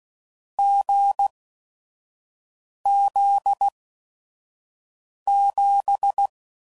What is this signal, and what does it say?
Morse code, heard as a keyed tone, about 16 wpm, spaced as for 5 wpm, 800 Hz. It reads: GZ7
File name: Combinations / Difficult Combos / awj1 ndb6 gz7